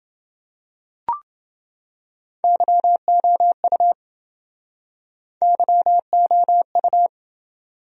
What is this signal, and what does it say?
Morse code, heard as a keyed tone, about 29 wpm, 700 Hz. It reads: YOU YOU